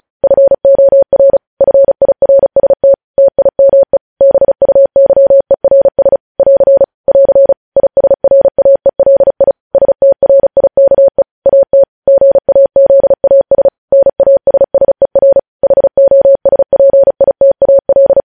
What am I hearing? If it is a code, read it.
Morse code, heard as a keyed tone, about 36 wpm, 550 Hz. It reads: FOR FIRST TIME BUYERS <AR> <AR> ISRAELI STRIKE AT GAZAS NASSER HOSPITAL
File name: podcast / News headlines-1x-35x35